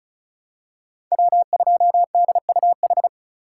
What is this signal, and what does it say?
Morse code, heard as a keyed tone, about 34 wpm, 700 Hz. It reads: W2DUH
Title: USA Call Signs - 5 Characters - 4x Character Spacing - 35wpm